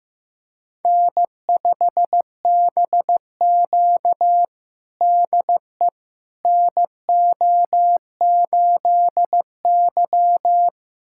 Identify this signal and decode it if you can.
Morse code, heard as a keyed tone, about 15 wpm, 700 Hz. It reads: N5BQ DE NO8Y
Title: USA Callsign DE Callsign - 4 Characters - 15wpm